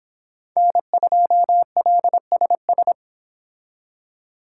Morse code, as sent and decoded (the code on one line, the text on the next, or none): -. ..--- .-.. ... ...
N2LSS